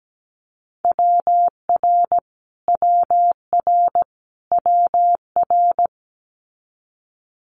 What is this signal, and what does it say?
Morse code, heard as a keyed tone, about 17 wpm, 700 Hz. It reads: WR WR WR
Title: Multi-letter Phonemes Repeated 3x 17wpm